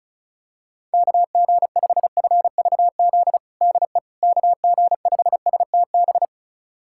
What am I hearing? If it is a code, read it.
Morse code, heard as a keyed tone, about 36 wpm, 700 Hz. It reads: KG5FVZ DE KG5STB
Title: USA Callsign DE Callsign - 6 Characters - 35wpm